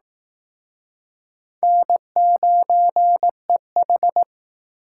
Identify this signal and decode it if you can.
Morse code, heard as a keyed tone, about 19 wpm, 700 Hz. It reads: N9EH